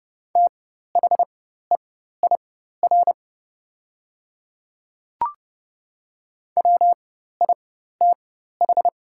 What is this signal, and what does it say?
Morse code, heard as a keyed tone, about 30 wpm, spaced as for 14 wpm, 700 Hz. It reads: THEIR WITH
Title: 4x Character Spacing - Top 100 Words - 30wpm